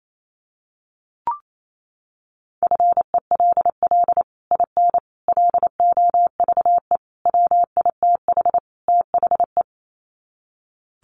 Morse code, heard as a keyed tone, about 28 wpm, 700 Hz. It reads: FELL IN LOVE WITH THE
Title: Sets of 5 Words Encore2 28wpm